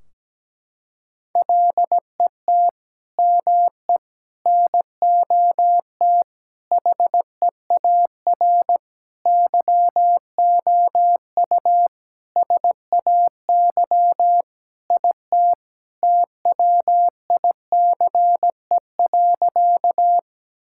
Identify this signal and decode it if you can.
Morse code, heard as a keyed tone, about 17 wpm, 700 Hz. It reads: LET ME NOT HEAR YOU SAY IT TWICE.
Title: Sentences from Top 1600 Words 17wpm